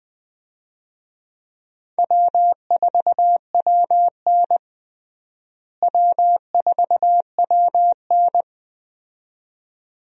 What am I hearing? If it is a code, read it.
Morse code, heard as a keyed tone, about 20 wpm, 700 Hz. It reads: W4WN W4WN
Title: USA Call Signs - 4 Characters - Repeated 2x - 20wpm